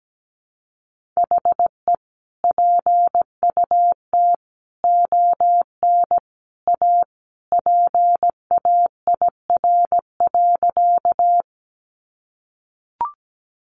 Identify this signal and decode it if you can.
Morse code, heard as a keyed tone, about 17 wpm, 700 Hz. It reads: HE PUT ON A PAIR.
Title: Sentences from Top 1200 Words 17wpm